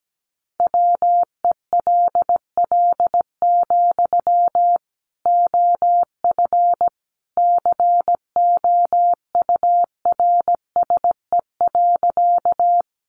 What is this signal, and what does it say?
Morse code, heard as a keyed tone, about 17 wpm, 700 Hz. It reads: WELL, OF COURSE.
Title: Sentences from Top 500 Words 17wpm